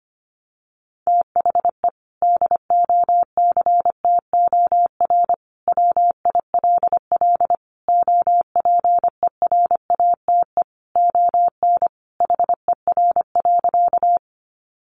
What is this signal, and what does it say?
Morse code, heard as a keyed tone, about 25 wpm, 700 Hz. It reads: THE DOCTOR WILL OPERATE ON HER.
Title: Sentences from Top 1500 Words 25wpm